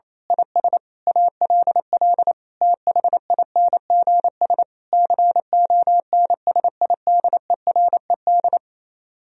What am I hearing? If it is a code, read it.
Morse code, heard as a keyed tone, about 26 wpm, 700 Hz. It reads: IS ALL THINGS CONSIDERED
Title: Speed-Racing - Sets of 5 Words 28wpm